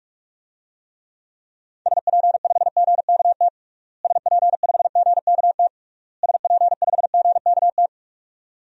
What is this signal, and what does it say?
Morse code, heard as a keyed tone, about 46 wpm, 700 Hz. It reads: SP5GKT SP5GKT SP5GKT
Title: DXCC-100 Call Signs - 6 Characters - Repeated 3x - 45wpm